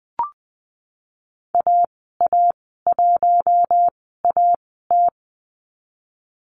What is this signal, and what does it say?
Morse code, heard as a keyed tone, about 20 wpm, spaced as for 15 wpm, 700 Hz. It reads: AA1AT
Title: 055 Lesson 10g - Letter r - Call Signs 20wpm